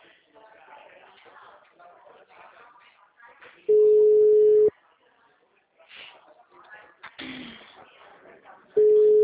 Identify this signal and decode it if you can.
Ringback tone (Europe, 425 Hz), with speech.